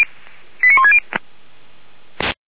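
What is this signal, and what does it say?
Five-tone selective call (ZVEI), heard as quick runs of five tones, tones of about 80 ms. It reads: 89160
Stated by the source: ZVEI five-tone selective call received with an SDR.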